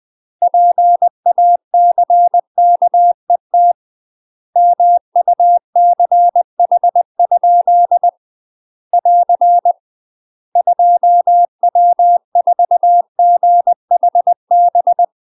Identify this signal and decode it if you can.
Morse code, heard as a keyed tone, about 20 wpm, 700 Hz. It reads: PACKET MUCH? <AR> 2W4GHB
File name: UK MASTER QSOs TRAINING / 20WPM / UK MASTER QSOs 0018